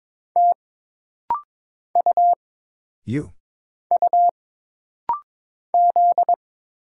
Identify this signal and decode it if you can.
Morse code, heard as a keyed tone, about 22 wpm, 700 Hz. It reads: T U U Z
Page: This rapid-fire timing is geared to push your ICR (Instant Character Recognition) skills.